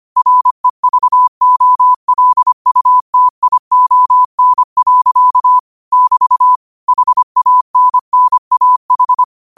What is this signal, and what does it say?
Morse code, heard as a keyed tone, about 25 wpm, 1000 Hz. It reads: REVOLUTION. <BT> HANNAH
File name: qotdfiles / 25wpm